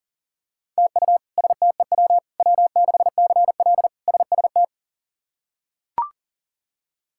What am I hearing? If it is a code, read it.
Morse code, heard as a keyed tone, about 40 wpm, 700 Hz. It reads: TU STEW W6CL SST
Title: K1USN SST QSO Practice - 40wpm